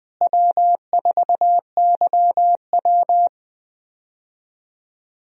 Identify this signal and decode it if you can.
Morse code, heard as a keyed tone, about 20 wpm, 700 Hz. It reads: W4YW
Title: USA Call Signs - 4 Characters - Repeated 2x - 20wpm